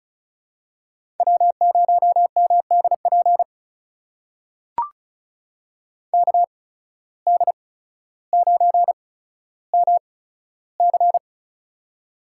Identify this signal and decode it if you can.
Morse code, heard as a keyed tone, about 36 wpm, spaced as for 15 wpm, 700 Hz. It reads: W0MDP KD9MC
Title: USA Call Signs - 5 Characters - 8x Character Spacing - 35wpm